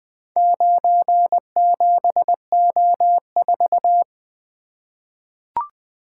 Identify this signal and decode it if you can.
Morse code, heard as a keyed tone, about 20 wpm, 700 Hz. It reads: 97O4
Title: Four Letter-Number Combinations - 20wpm